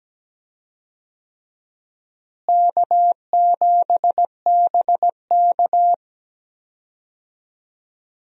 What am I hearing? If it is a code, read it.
Morse code, heard as a keyed tone, about 17 wpm, 700 Hz. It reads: K7BK
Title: USA Call Signs - 4 Characters - Repeated 3x - 17wpm